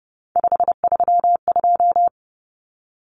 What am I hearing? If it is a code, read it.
Morse code, heard as a keyed tone, about 30 wpm, 700 Hz. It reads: 532
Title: Numbers - 3-Digit 30wpm